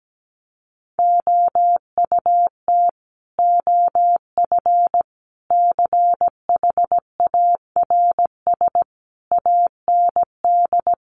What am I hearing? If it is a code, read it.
Morse code, heard as a keyed tone, about 17 wpm, 700 Hz. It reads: OUT OF CHARS AND
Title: A Sound Of Thunder 17wpm